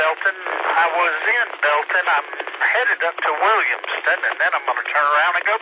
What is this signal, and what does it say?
Emergency Alert System (SAME) tones, heard as harsh buzzing bursts, with speech. It is heard from 1.1 to 4.6 s.